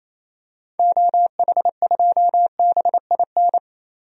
Morse code, heard as a keyed tone, about 28 wpm, 700 Hz. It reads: OH2BIN
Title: DXCC-100 Call Signs - 6 Characters - Repeated 3x - 28wpm